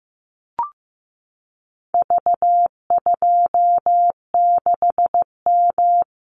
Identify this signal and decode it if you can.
Morse code, heard as a keyed tone, about 15 wpm, 700 Hz. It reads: V26M
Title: Call Signs Repeated 2x - 15wpm